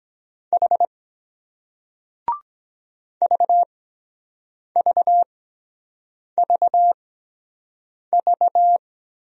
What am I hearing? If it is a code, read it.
Morse code, heard as a keyed tone, about 22 wpm, spaced as for 7 wpm, 700 Hz. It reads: H VVVV